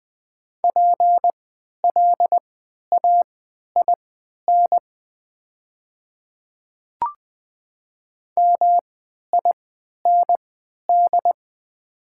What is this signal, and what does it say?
Morse code, heard as a keyed tone, about 20 wpm, spaced as for 11 wpm, 700 Hz. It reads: PLAIN MIND
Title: 3x Character Spacing - Top 500 Words - 20wpm